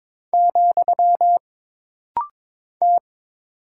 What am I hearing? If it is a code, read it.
Morse code, heard as a keyed tone, about 22 wpm, 700 Hz. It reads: , T
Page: This rapid-fire timing is geared to push your ICR (Instant Character Recognition) skills.